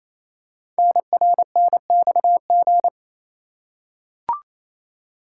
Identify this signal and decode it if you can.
Morse code, heard as a keyed tone, about 27 wpm, 700 Hz. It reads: NRNXG
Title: Code Groups - 5 Letters - 28wpm